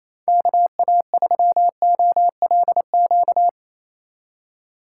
Morse code, heard as a keyed tone, about 34 wpm, 700 Hz. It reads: KA3OLQ
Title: Speed Racing - USA 2x3 Call Signs 28wpm